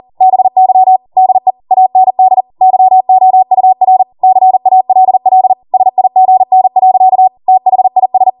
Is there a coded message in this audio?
Yes: Morse code from 0.2 to 8.3 s, heard as a keyed tone, about 40 wpm, 750 Hz. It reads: BY DE AND YOUR CALL SIGN. THIS